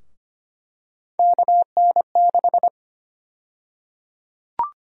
Morse code, heard as a keyed tone, about 25 wpm, 700 Hz. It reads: KN6
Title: USA Call Sign Prefixes - Repeated 2x - 25wpm